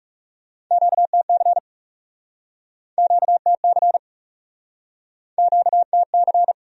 Morse code, heard as a keyed tone, about 42 wpm, 700 Hz. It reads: QTC QTC QTC